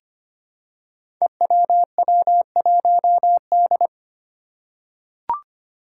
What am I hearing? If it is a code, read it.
Morse code, heard as a keyed tone, about 25 wpm, 700 Hz. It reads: EWW1D
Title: Five Challenging Letter-Number Combinations 25wpm